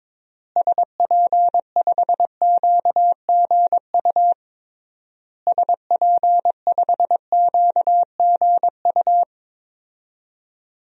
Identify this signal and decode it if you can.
Morse code, heard as a keyed tone, about 22 wpm, 700 Hz. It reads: SP5QGU SP5QGU
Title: DXCC-100 Call Signs - 6 Characters - Repeated 3x - 22wpm